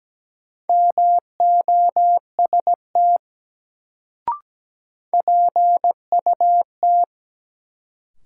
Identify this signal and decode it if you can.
Morse code, heard as a keyed tone, about 17 wpm, 700 Hz. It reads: MOST PUT
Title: Sentences from Top 200 Words - New Word Drill - 17wpm